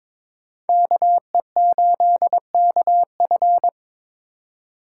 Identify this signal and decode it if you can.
Morse code, heard as a keyed tone, about 26 wpm, 700 Hz. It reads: KE8KF
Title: Speed Racing - USA 2x2 Call Signs 22wpm